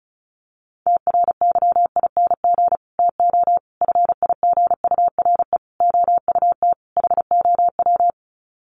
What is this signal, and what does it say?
Morse code, heard as a keyed tone, about 34 wpm, 700 Hz. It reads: TRYING TO FIGURE OUT HOW
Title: Sets of 5 Words Encore 35wpm